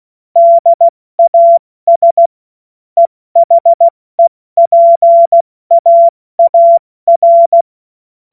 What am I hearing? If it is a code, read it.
Morse code, heard as a keyed tone, about 16 wpm, 680 Hz. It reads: DAS EHEPAAR
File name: Morsekurs / Witze / Witz 15. August 2023 70 F80